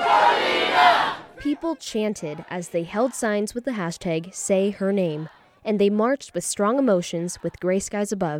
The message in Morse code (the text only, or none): none